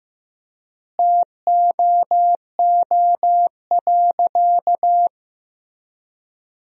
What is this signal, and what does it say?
Morse code, heard as a keyed tone, about 15 wpm, 700 Hz. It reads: TOO.
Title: 3x Word-Spacing - Sentences from Top 500 Words - 15wpm